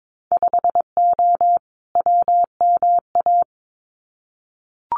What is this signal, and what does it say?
Morse code, heard as a keyed tone, about 22 wpm, 700 Hz. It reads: 5O WMA
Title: Winter Field Day Exchanges 22wpm